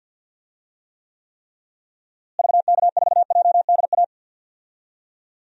Morse code, heard as a keyed tone, about 50 wpm, 700 Hz. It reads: VK4JDA
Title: DXCC-100 Call Signs - 6 Characters - Repeated 3x - 50wpm